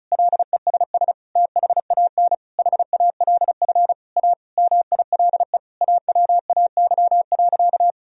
Morse code, heard as a keyed tone, about 36 wpm, 700 Hz. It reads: LESS THAN HALF A MILE AWAY.
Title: Sentences from Top 1000 Words - Review Part 2 - 35wpm